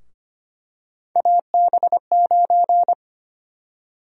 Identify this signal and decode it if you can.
Morse code, heard as a keyed tone, about 25 wpm, 700 Hz. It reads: AB9
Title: USA Call Sign Prefixes - Repeated 2x - 25wpm